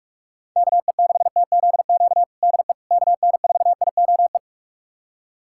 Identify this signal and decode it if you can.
Morse code, heard as a keyed tone, about 46 wpm, 700 Hz. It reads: KE6TZQ DE KN4IOE
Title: USA Callsign DE Callsign 45wpm